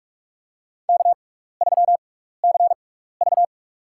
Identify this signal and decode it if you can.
Morse code, heard as a keyed tone, about 42 wpm, spaced as for 15 wpm, 700 Hz. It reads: K3CV
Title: USA Call Signs - 4 Characters - 6x Character Spacing - 45wpm